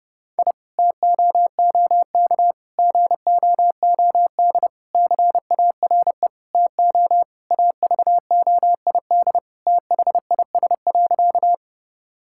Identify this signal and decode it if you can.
Morse code, heard as a keyed tone, about 32 wpm, 700 Hz. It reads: I TOOK GOOD CARE TO AVOID THIS.
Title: Sentences from Top 1000 Words - Review Part 6 - 30wpm